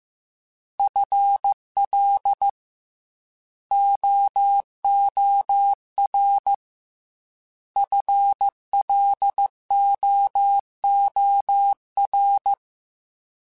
Morse code, heard as a keyed tone, about 15 wpm, 800 Hz. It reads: FL OOR FLOOR
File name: FL / FL mp3s / Fast FL @13